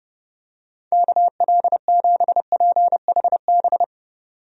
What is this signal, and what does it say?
Morse code, heard as a keyed tone, about 32 wpm, 700 Hz. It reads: KL7PHB
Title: DXCC-100 Call Signs - 6 Characters - 30wpm